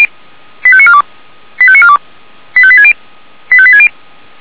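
Five-tone selective call (ZVEI), heard as quick runs of five tones, tones of about 70 ms. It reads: 85942 85942 85860 85860
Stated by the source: ZVEI five-tone selective call received on SDR.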